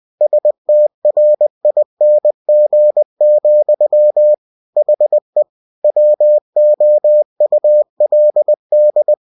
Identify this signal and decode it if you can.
Morse code, heard as a keyed tone, about 20 wpm, 600 Hz. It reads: STRING, HE WOULD